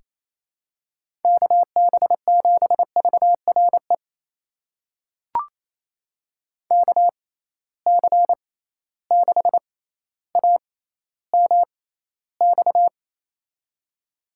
Morse code, heard as a keyed tone, about 28 wpm, spaced as for 14 wpm, 700 Hz. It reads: KB7VRE KC6AMX
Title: USA Call Signs - 6 Characters - 6x Character Spacing - 28wpm